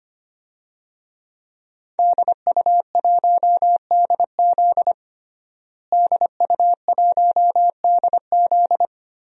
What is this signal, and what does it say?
Morse code, heard as a keyed tone, about 25 wpm, 700 Hz. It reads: DU1DZ DU1DZ
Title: DXCC-100 Call Signs - 5 Characters - Repeated 3x - 25wpm